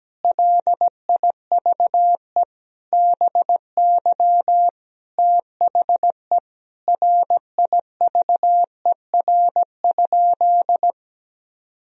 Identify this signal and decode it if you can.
Morse code, heard as a keyed tone, about 17 wpm, 700 Hz. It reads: LIVE BY THE RIVER?
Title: Sentences from Top 1000 Words - Review Part 1 - 17wpm